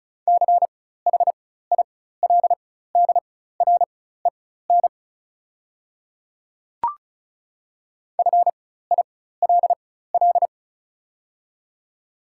Morse code, heard as a keyed tone, about 36 wpm, spaced as for 16 wpm, 690 Hz. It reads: CHILDREN FILL
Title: 4x Character Spacing - Top 500 Words - 35wpm